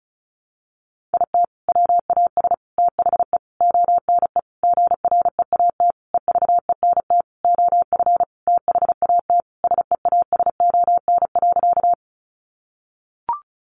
Morse code, heard as a keyed tone, about 34 wpm, 700 Hz. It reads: IT WAS THE ONE GREAT EVENT OF THAT SEASON.
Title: Sentences from Top 1400 Words 35wpm